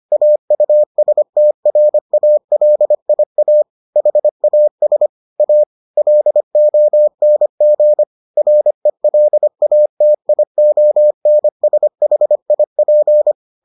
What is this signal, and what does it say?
Morse code, heard as a keyed tone, about 25 wpm, 600 Hz. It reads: AUSTRALIA HAS A LONG RELATIONSHIP